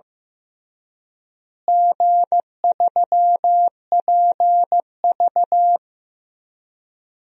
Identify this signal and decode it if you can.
Morse code, heard as a keyed tone, about 16 wpm, 700 Hz. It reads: G3PV
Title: DXCC-100 Call Signs - 4 Characters - Repeated 3x - 15wpm